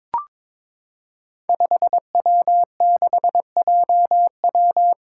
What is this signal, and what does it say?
Morse code, heard as a keyed tone, about 22 wpm, 700 Hz. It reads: 5W6JW
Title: Five Challenging Letter-Number Combinations 22wpm